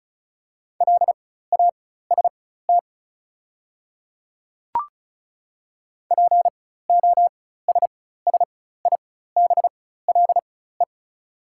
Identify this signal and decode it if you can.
Morse code, heard as a keyed tone, about 36 wpm, spaced as for 16 wpm, 700 Hz. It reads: LAST POSSIBLE
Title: 4x Character Spacing - Top 500 Words - 35wpm